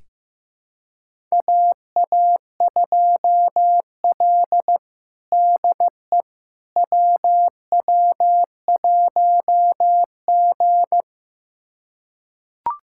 Morse code, heard as a keyed tone, about 15 wpm, 700 Hz. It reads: AA2L DE WW1G
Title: USA Callsign DE Callsign - 4 Characters - 15wpm